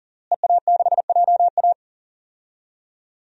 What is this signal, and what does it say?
Morse code, heard as a keyed tone, about 40 wpm, 700 Hz. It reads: EA6JA
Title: DXCC-100 Call Signs - 5 Characters - Repeated 2x - 40wpm